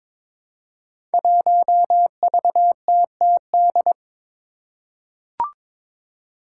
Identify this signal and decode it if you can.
Morse code, heard as a keyed tone, about 22 wpm, 700 Hz. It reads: 1VTTD
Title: Five Challenging Letter-Number Combinations 22wpm